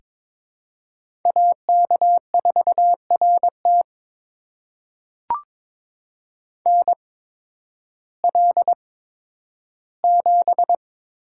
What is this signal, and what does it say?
Morse code, heard as a keyed tone, about 22 wpm, spaced as for 12 wpm, 700 Hz. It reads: AK4RT NL7
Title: USA Call Signs - 5 Characters - 8x Character Spacing - 22wpm